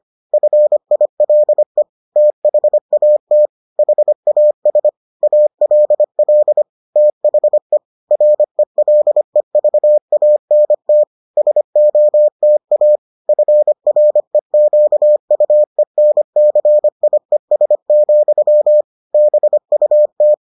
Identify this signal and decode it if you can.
Morse code, heard as a keyed tone, about 25 wpm, 600 Hz. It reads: FILE THAT HAS ALL THE RELEVANT SOTA FREQUENCIES, BUT